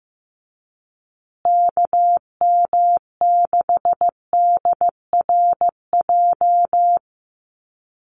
Morse code, heard as a keyed tone, about 15 wpm, 700 Hz. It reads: KM6DRJ
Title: USA Call Signs Repeated 3x 15wpm